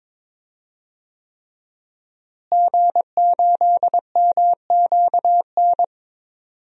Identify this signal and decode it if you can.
Morse code, heard as a keyed tone, about 22 wpm, 700 Hz. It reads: G8MQN
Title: DXCC-100 Call Signs - 5 Characters - Repeated 3x - 22wpm